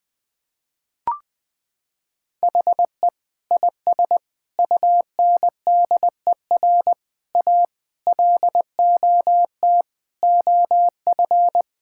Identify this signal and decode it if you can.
Morse code, heard as a keyed tone, about 20 wpm, 700 Hz. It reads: HE IS UNDER A LOT OF
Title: Sentences from Top 1000 Words - Review Part 6 - 20wpm